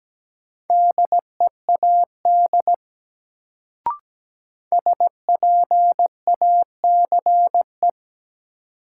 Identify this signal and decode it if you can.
Morse code, heard as a keyed tone, about 17 wpm, 700 Hz. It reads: DEAD SPACE
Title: Sentences from Top 1500 Words - New Word Drill Alt - 17wpm